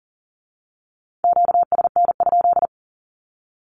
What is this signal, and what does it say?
Morse code, heard as a keyed tone, about 40 wpm, 690 Hz. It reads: QSN?
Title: Q-Codes with Spoken Meaning - 40wpm